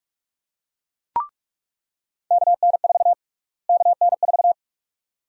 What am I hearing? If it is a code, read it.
Morse code, heard as a keyed tone, about 42 wpm, 700 Hz. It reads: KN4 KN4
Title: USA Call Sign Prefixes - Repeated 2x - 45wpm